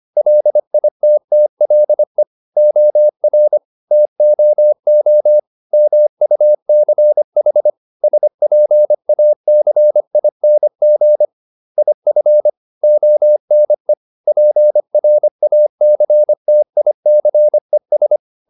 Morse code, heard as a keyed tone, about 25 wpm, 600 Hz. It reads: LITTLE OR TOO MUCH SPACING IF ONE PRACTICES